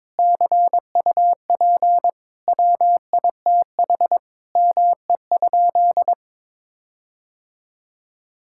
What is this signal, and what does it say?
Morse code, heard as a keyed tone, about 21 wpm, 700 Hz. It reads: CUP WITH ME?
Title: Sentences from Top 1600 Words 22wpm